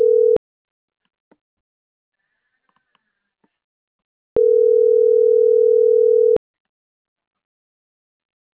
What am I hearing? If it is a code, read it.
Ringback tone (North America), 440 + 480 Hz.
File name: CENTRAL NAVIERAS / out onelinecapacitacion - .27825